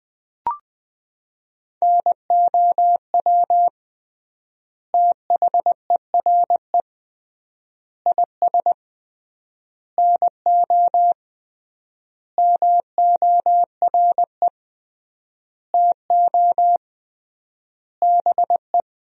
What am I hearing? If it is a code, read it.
Morse code, heard as a keyed tone, about 20 wpm, 700 Hz. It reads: NOW THERE IS NO MORE TO BE
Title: 3x Word-Spacing - Sentences from Top 100 Words - 20wpm